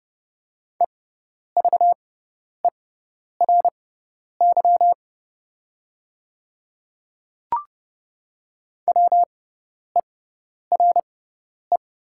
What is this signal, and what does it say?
Morse code, heard as a keyed tone, about 30 wpm, spaced as for 10 wpm, 700 Hz. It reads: EVERY WERE
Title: 6x Character Spacing - Top 300 Words - 30wpm